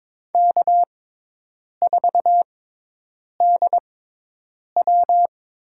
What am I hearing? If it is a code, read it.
Morse code, heard as a keyed tone, about 22 wpm, spaced as for 8 wpm, 700 Hz. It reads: K4DW
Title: USA Call Signs - 6x Character Spacing - 22wpm